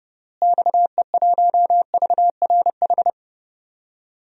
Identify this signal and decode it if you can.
Morse code, heard as a keyed tone, about 30 wpm, 700 Hz. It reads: XE1VRH
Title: DXCC-100 Call Signs - 6 Characters - Repeated 3x - 30wpm